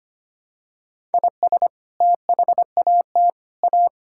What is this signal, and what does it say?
Morse code, heard as a keyed tone, about 25 wpm, 700 Hz. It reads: IS THAT A
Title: Sentences from Top 1000 Words - Review Part 5 - 25wpm